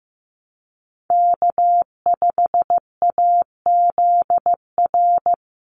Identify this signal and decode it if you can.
Morse code, heard as a keyed tone, about 15 wpm, 700 Hz. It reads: K5AZR
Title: Call Signs Repeated 3x - 15wpm